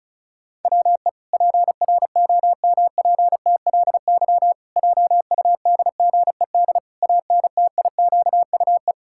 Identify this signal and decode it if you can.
Morse code, heard as a keyed tone, about 34 wpm, 700 Hz. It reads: WE PROMPTLY JUDGED ANTIQUE
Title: Pangram Challenge v2 - 35wpm